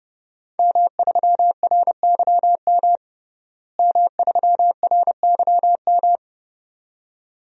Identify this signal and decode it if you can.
Morse code, heard as a keyed tone, about 30 wpm, 700 Hz. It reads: M3RYM M3RYM
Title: DXCC-100 Call Signs - 5 Characters - Repeated 2x - 30wpm